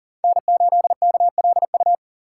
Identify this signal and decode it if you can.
Morse code, heard as a keyed tone, about 40 wpm, 700 Hz. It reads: N8KLU
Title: USA Call Signs - 5 Characters - 6x Character Spacing - 40wpm